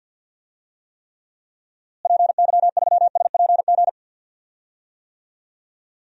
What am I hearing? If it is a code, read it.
Morse code, heard as a keyed tone, about 52 wpm, 700 Hz. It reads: PY3SPG
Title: DXCC-100 Call Signs - 6 Characters - Repeated 2x - 50wpm